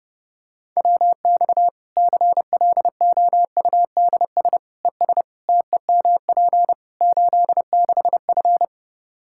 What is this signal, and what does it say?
Morse code, heard as a keyed tone, about 30 wpm, 700 Hz. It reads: WX CLOUDS ES TEMP 86F
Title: Practice QSO - Exchange 2 - Info - 30wpm